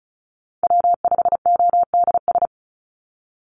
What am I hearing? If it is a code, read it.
Morse code, heard as a keyed tone, about 34 wpm, 700 Hz. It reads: W5ODS
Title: NA SOTA Activator Callsigns 35wpm